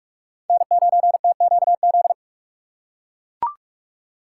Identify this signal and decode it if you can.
Morse code, heard as a keyed tone, about 44 wpm, 690 Hz. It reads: N9TQZ
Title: USA Call Signs - 5 Characters - 6x Character Spacing - 45wpm